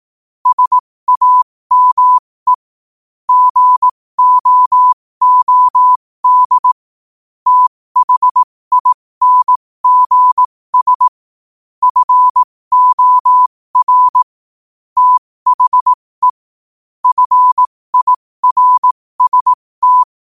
Morse code, heard as a keyed tone, about 18 wpm, spaced as for 15 wpm, 1000 Hz. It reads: SAME GOOD THINGS FOR THE FIRST
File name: qotdfiles / 15wpm